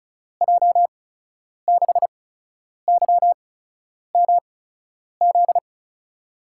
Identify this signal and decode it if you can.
Morse code, heard as a keyed tone, about 36 wpm, spaced as for 10 wpm, 700 Hz. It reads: J6YMZ
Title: USA Call Signs - 6 Characters - 8x Character Spacing - 35wpm